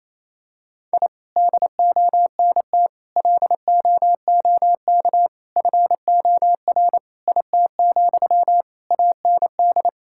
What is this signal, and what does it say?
Morse code, heard as a keyed tone, about 28 wpm, 700 Hz. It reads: I DONT LOOK FOR IT, AND
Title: Sentences from Top 1000 Words - Review Part 5 - 28wpm